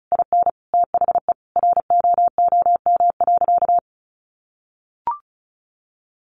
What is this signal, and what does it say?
Morse code, heard as a keyed tone, about 34 wpm, 700 Hz. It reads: IN THE ROOM.
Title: A Sound Of Thunder 35wpm